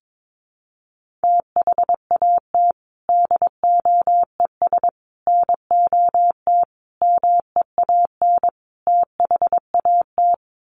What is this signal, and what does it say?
Morse code, heard as a keyed tone, about 22 wpm, 700 Hz. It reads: THAT DOES NOT MEAN THAT
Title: Sets of 5 Words Encore 22wpm